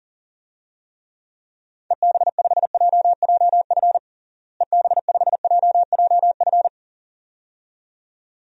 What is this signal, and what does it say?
Morse code, heard as a keyed tone, about 37 wpm, 700 Hz. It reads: EB5JJF EB5JJF
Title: DXCC-100 Call Signs - 6 Characters - Repeated 2x - 40wpm